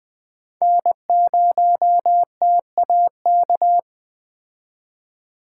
Morse code, heard as a keyed tone, about 21 wpm, 700 Hz. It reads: N0TAK
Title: Speed Racing - USA 1x3 Call Signs 17wpm